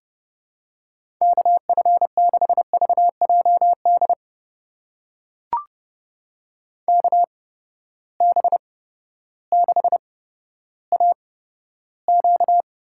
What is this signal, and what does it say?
Morse code, heard as a keyed tone, about 31 wpm, spaced as for 14 wpm, 700 Hz. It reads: KF6VJD KB6AQ
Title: USA Call Signs - 6 Characters - 8x Character Spacing - 30wpm